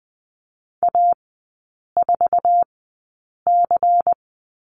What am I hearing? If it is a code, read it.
Morse code, heard as a keyed tone, about 20 wpm, spaced as for 8 wpm, 700 Hz. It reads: A4C
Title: 093 Lesson 16e - Letter c - Three Characters 20wpm